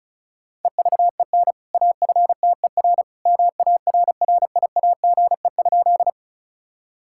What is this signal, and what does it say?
Morse code, heard as a keyed tone, about 36 wpm, 700 Hz. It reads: EVEN AFTER MARRIAGE?
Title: Sentences from Top 1000 Words - Review Part 5 - 35wpm